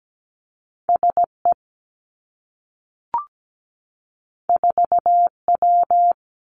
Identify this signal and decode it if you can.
Morse code, heard as a keyed tone, about 17 wpm, 700 Hz. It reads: SE 4W
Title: Two Challenging Letter-Number Combinations 17wpm